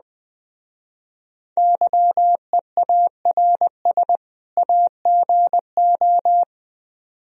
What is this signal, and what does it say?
Morse code, heard as a keyed tone, about 19 wpm, 700 Hz. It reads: YEARS AGO
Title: Speed-Racing - Sets of 2 Words 17wpm